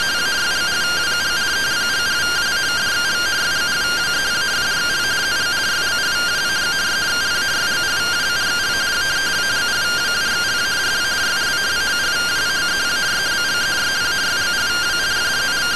Two-tone teleprinter signal, about 170 Hz shift.
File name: findings / MR AXIAL 14.gen-00 rtty rx